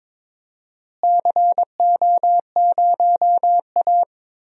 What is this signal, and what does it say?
Morse code, heard as a keyed tone, about 22 wpm, 700 Hz. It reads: CO0A
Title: DXCC-100 Call Signs - 4 Characters - Repeated 2x - 22wpm